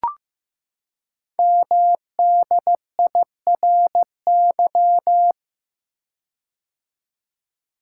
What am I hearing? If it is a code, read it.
Morse code, heard as a keyed tone, about 15 wpm, 700 Hz. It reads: MDIRY
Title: Code Groups - 5 Letters - 15wpm